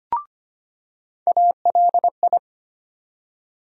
Morse code, heard as a keyed tone, about 25 wpm, 700 Hz. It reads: ALI
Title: USA Call Sign Postfixes - 3 Letters 25wpm